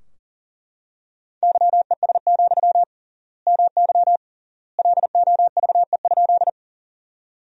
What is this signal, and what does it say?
Morse code, heard as a keyed tone, about 40 wpm, 700 Hz. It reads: YES, MY LOVE?
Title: 3x Word-Spacing - Sentences from Top 500 Words - 40wpm